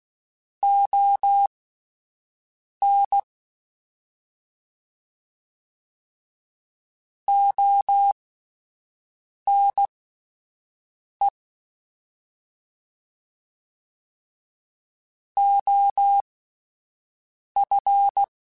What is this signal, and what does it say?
Morse code, heard as a keyed tone, about 16 wpm, spaced as for 5 wpm, 800 Hz. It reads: ON ONE OF
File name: Alphabetical / Alphabetical mp3s / Os @5